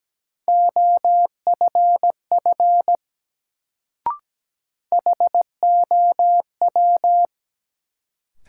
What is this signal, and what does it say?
Morse code, heard as a keyed tone, about 17 wpm, 700 Hz. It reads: OFF HOW
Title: Sets of 2 Words - New Word Drill - 17wpm